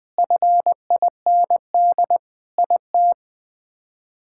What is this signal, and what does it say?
Morse code, heard as a keyed tone, about 20 wpm, 700 Hz. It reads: FIND IT
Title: Sentences with 3-Word Phrases - Spoken Beginning - 20wpm